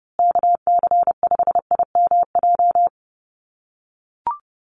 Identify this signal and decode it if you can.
Morse code, heard as a keyed tone, about 30 wpm, 700 Hz. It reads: KC5IMJ
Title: USA Call Signs Repeated 3x 30wpm